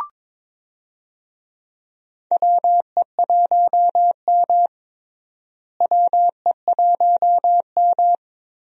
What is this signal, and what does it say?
Morse code, heard as a keyed tone, about 22 wpm, 700 Hz. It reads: WE1M WE1M
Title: USA Call Signs - 4 Characters - Repeated 3x - 22wpm